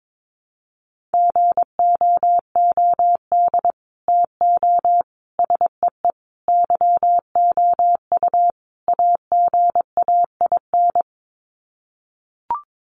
Morse code, heard as a keyed tone, about 22 wpm, 700 Hz. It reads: GOOD TO SEE YOU AGAIN
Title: Sets of 5 Words Encore2 22wpm